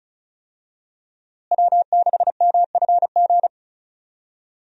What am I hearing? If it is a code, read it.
Morse code, heard as a keyed tone, about 36 wpm, 700 Hz. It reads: W6MFG
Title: USA Call Signs - 5 Characters - Repeated 3x - 35wpm